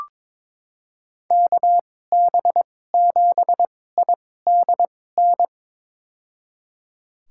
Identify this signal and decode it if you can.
Morse code, heard as a keyed tone, about 22 wpm, spaced as for 16 wpm, 700 Hz. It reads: KB7IDN
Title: USA Call Signs - 6 Characters - 2x Character Spacing - 22wpm